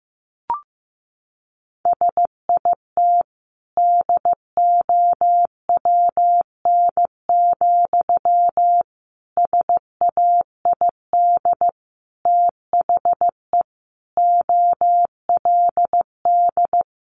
Morse code, heard as a keyed tone, about 15 wpm, 700 Hz. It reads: SIT DOWN, SAID THE OLD
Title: Sentences from Top 500 Words 15wpm